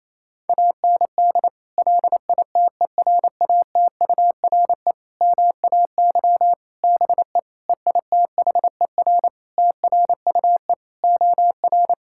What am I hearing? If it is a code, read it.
Morse code, heard as a keyed tone, about 28 wpm, 700 Hz. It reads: AND LITERATURE MAY BE EITHER TRUE OR
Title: Sentences from Top 1600 Words 28wpm